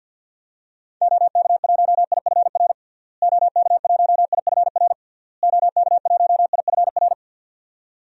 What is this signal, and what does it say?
Morse code, heard as a keyed tone, about 48 wpm, 700 Hz. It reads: OK1IFR OK1IFR OK1IFR
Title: DXCC-100 Call Signs - 6 Characters - Repeated 3x - 50wpm